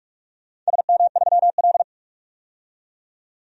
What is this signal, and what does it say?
Morse code, heard as a keyed tone, about 46 wpm, 700 Hz. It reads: SM3L
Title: DXCC-100 Call Signs - 4 Characters - 45wpm